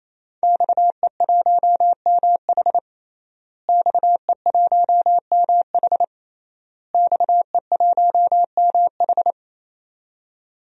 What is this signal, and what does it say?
Morse code, heard as a keyed tone, about 27 wpm, 700 Hz. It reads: XE1MH XE1MH XE1MH
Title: DXCC-100 Call Signs - 5 Characters - Repeated 3x - 28wpm